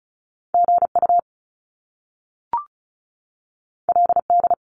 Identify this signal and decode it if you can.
Morse code, heard as a keyed tone, about 36 wpm, 700 Hz. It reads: GU LD
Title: Top Two Letter Combinations in Words 35wpm